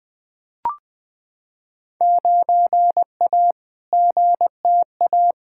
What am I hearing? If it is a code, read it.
Morse code, heard as a keyed tone, about 20 wpm, 700 Hz. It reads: 9A GTA
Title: Field Day Exchanges Encore - 20wpm